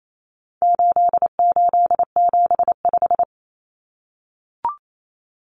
Morse code, heard as a keyed tone, about 28 wpm, 700 Hz. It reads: 8875
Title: Common Cut Numbers - 4-Digit 28wpm